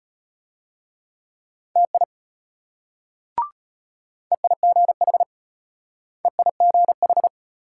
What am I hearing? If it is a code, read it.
Morse code, heard as a keyed tone, about 32 wpm, 700 Hz. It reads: TI EIGH EIGH